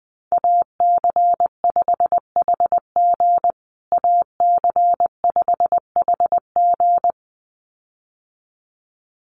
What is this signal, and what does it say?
Morse code, heard as a keyed tone, about 20 wpm, 700 Hz. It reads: AC5HG AC5HG
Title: USA Call Signs Repeated 2x 20wpm